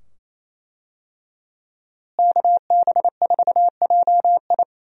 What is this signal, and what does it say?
Morse code, heard as a keyed tone, about 27 wpm, 700 Hz. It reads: KB4JI
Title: USA Call Signs - 5 Characters - Repeated 2x - 28wpm